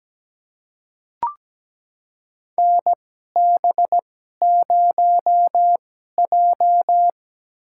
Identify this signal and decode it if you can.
Morse code, heard as a keyed tone, about 17 wpm, spaced as for 12 wpm, 700 Hz. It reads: NB0J
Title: USA Call Signs - 2x Character Spacing - 17wpm